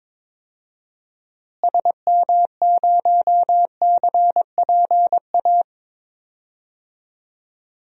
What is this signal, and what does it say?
Morse code, heard as a keyed tone, about 22 wpm, 700 Hz. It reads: SM0CPA
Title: DXCC-100 Call Signs - 6 Characters - Repeated 3x - 22wpm